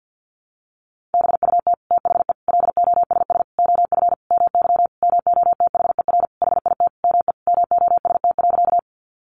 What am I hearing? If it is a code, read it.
Morse code, heard as a keyed tone, about 50 wpm, 700 Hz. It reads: BUT THE LOSS OF MY MOTHER HIT ME MOST.
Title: Sentences from Top 1400 Words 50wpm